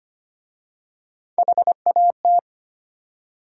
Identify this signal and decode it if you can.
Morse code, heard as a keyed tone, about 25 wpm, 700 Hz. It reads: HAT
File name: practice / 25wpm / Easy Words with Hints - 25wpm